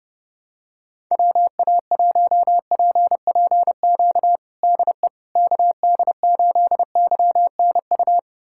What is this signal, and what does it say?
Morse code, heard as a keyed tone, about 28 wpm, 700 Hz. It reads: WA1PPQ DE KD8YNU
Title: USA Callsign DE Callsign - 6 Characters - 30wpm